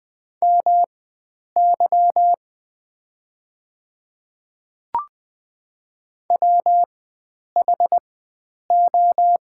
Morse code, heard as a keyed tone, about 20 wpm, spaced as for 9 wpm, 700 Hz. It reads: MY WHO
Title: 4x Character Spacing - Top 100 Words - 20wpm